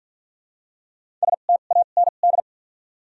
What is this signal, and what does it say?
Morse code, heard as a keyed tone, about 53 wpm, spaced as for 37 wpm, 700 Hz. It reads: STAND
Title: 2x Character Spacing - Top 300 Words - 50wpm